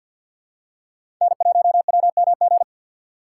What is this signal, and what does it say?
Morse code, heard as a keyed tone, about 52 wpm, 700 Hz. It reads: N1WMG
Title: USA Call Signs - 3x Character Spacing - 50wpm